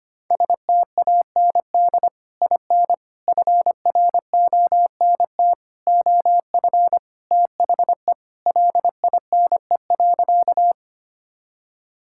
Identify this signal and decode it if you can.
Morse code, heard as a keyed tone, about 25 wpm, 700 Hz. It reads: STAND IN FRONT OF THE LINE.
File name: practice / 25wpm / Sentences with 3-Word Phrases - 25wpm